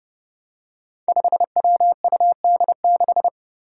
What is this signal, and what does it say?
Morse code, heard as a keyed tone, about 30 wpm, 700 Hz. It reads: 5WUD6
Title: Five Challenging Letter-Number Combinations 30wpm